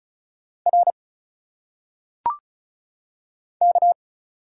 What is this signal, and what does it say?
Morse code, heard as a keyed tone, about 36 wpm, 700 Hz. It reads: R K
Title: Single Letters - Phonetic Pronunciation 35wpm